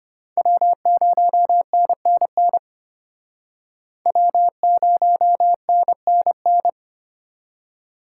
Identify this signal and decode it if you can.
Morse code, heard as a keyed tone, about 34 wpm, 700 Hz. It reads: W0NNN W0NNN